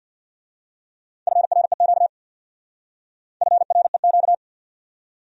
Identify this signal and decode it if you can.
Morse code, heard as a keyed tone, about 59 wpm, 700 Hz. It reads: FREQ FREQ
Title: Speed-Racing Abbreviations - 50wpm